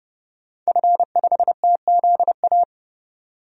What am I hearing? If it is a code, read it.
Morse code, heard as a keyed tone, about 32 wpm, 700 Hz. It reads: F5TZA
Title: DXCC-100 Call Signs - 5 Characters - 30wpm